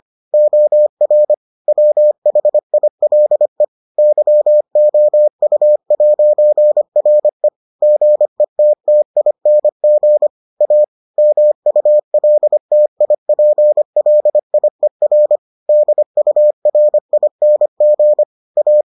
Morse code, heard as a keyed tone, about 25 wpm, 600 Hz. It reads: OR WHILE YOU'RE GETTING A MULTIPLIER DURING A